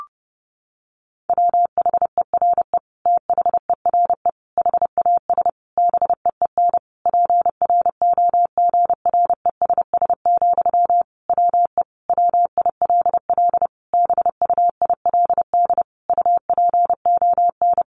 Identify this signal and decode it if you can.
Morse code, heard as a keyed tone, about 30 wpm, 700 Hz. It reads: WHERE THERE HAS BEEN PROGRESS, WE WILL BUILD UPON
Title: Sentences from Top 1400 Words 30wpm